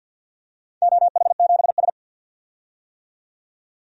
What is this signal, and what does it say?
Morse code, heard as a keyed tone, about 53 wpm, 700 Hz. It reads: OH7S
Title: DXCC-100 Call Signs - 4 Characters - 50wpm